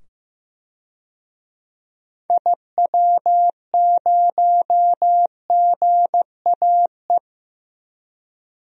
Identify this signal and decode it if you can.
Morse code, heard as a keyed tone, about 15 wpm, 700 Hz. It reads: IW0GAE
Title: DXCC-100 Call Signs - 6 Characters - 15wpm